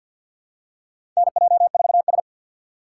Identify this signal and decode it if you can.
Morse code, heard as a keyed tone, about 52 wpm, 700 Hz. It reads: NJ4S
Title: USA Call Signs - 4 Characters - 3x Character Spacing - 50wpm